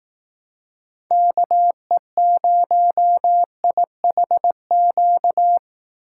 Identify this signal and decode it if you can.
Morse code, heard as a keyed tone, about 18 wpm, 700 Hz. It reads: KE0IHQ
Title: Speed Racing - USA 2x3 Call Signs 15wpm